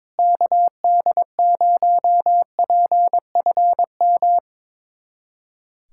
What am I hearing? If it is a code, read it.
Morse code, heard as a keyed tone, about 22 wpm, 700 Hz. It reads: KD0PFM
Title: USA Callsign DE Callsign - 6 Characters - 22wpm